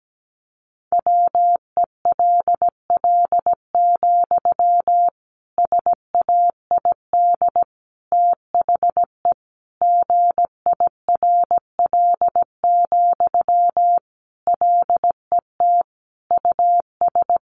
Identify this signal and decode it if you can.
Morse code, heard as a keyed tone, about 17 wpm, 700 Hz. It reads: WELL, SAID THE GIRL, LET US
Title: Sentences from Top 400 Words 17wpm